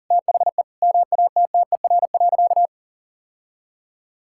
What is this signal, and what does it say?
Morse code, heard as a keyed tone, about 40 wpm, 700 Hz. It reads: THE MATTER.
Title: Sentences from Top 1000 Words - Review Part 5 - 40wpm